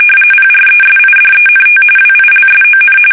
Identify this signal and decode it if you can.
RTTY, about 45.45 baud. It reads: PRESIDENT WILSON
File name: fsk850